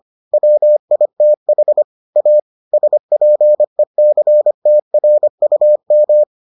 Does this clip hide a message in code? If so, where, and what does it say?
Yes: Morse code from 0.3 to 6.2 s, heard as a keyed tone, about 25 wpm, 600 Hz. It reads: WITH A SPECTRUM